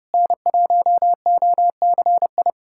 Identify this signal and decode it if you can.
Morse code, heard as a keyed tone, about 32 wpm, 700 Hz. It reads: N1OCI
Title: USA Callsign DE Callsign 30wpm